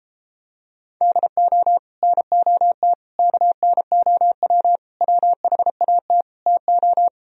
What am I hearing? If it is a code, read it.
Morse code, heard as a keyed tone, about 33 wpm, 700 Hz. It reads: DO NOT KNOW WHAT TO